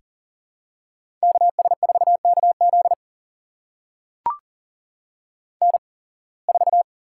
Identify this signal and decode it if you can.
Morse code, heard as a keyed tone, about 40 wpm, spaced as for 26 wpm, 700 Hz. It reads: KS4KZ N4
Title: USA Call Signs - 8x Character Spacing - 40wpm